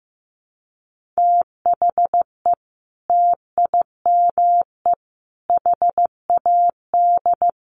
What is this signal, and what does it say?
Morse code, heard as a keyed tone, about 15 wpm, 700 Hz. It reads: THE TIME HAD
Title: Sentences from Top 100 Words 15wpm